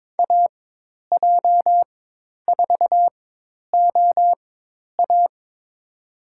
Morse code, heard as a keyed tone, about 22 wpm, spaced as for 10 wpm, 700 Hz. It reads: AJ4OA
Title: USA Call Signs - 5 Characters - 4x Character Spacing - 22wpm